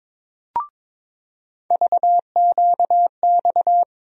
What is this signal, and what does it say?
Morse code, heard as a keyed tone, about 22 wpm, 700 Hz. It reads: VQX
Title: USA Call Sign Postfixes - 3 Letters 22wpm